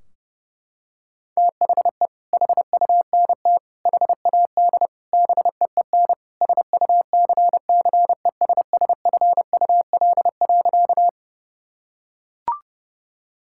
Morse code, heard as a keyed tone, about 32 wpm, 700 Hz. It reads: THE HUNT HAD BEEN SUCCESSFUL.
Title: Sentences from Top 1600 Words 30wpm